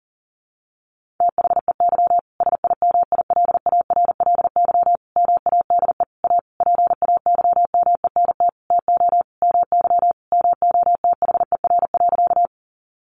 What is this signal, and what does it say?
Morse code, heard as a keyed tone, about 40 wpm, 700 Hz. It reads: THEY SIMILARLY MADE A PAYMENT TO MY MOTHER.
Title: Sentences from Top 1400 Words 40wpm